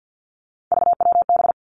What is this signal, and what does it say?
Morse code, heard as a keyed tone, about 50 wpm, 700 Hz. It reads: VWB
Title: Three Challenging Letter-Number Combinations 50wpm